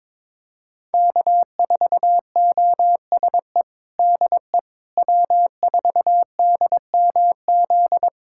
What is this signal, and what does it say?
Morse code, heard as a keyed tone, about 22 wpm, 700 Hz. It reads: K4OSE DE W4DMZ
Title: USA Callsign DE Callsign - 5 Characters - 22wpm